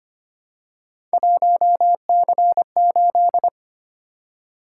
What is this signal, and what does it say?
Morse code, heard as a keyed tone, about 25 wpm, 700 Hz. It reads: 1C8
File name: Three Letter-Number Combinations - 25wpm